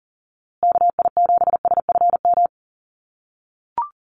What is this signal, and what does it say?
Morse code, heard as a keyed tone, about 39 wpm, 700 Hz. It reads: KI7SFM
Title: USA Call Signs Repeated 3x 40wpm